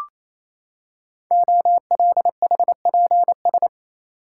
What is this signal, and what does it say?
Morse code, heard as a keyed tone, about 29 wpm, 700 Hz. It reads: OLHPS
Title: Code Groups - 5 Letters - 28wpm